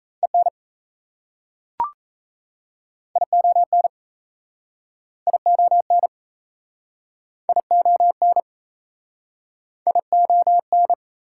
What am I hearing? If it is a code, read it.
Morse code, heard as a keyed tone, about 35 wpm, 700 Hz. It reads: EN ION ION ION ION
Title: Speed-Racing Prefixes and Suffixes 28wpm